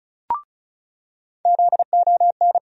Morse code, heard as a keyed tone, about 36 wpm, 700 Hz. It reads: ZON
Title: USA Call Sign Postfixes - 3 Letters 35wpm